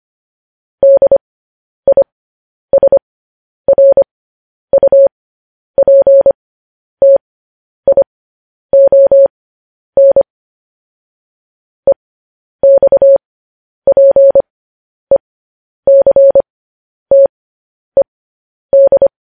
Morse code, heard as a keyed tone, about 25 wpm, spaced as for 10 wpm, 550 Hz. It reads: DISRUPTION EXPECTED
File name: podcast / News headlines-1x-25x10